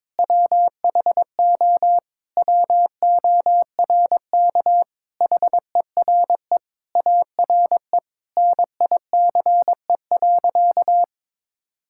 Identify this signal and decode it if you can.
Morse code, heard as a keyed tone, about 21 wpm, 700 Hz. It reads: WHO WORK HERE ARE NICE.
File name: practice / 22wpm / Sentences with 3-Word Phrases - 22wpm